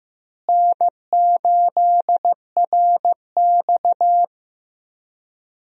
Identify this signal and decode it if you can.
Morse code, heard as a keyed tone, about 15 wpm, 700 Hz. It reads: N8RX
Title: USA Call Signs - 4 Characters - Repeated 3x - 15wpm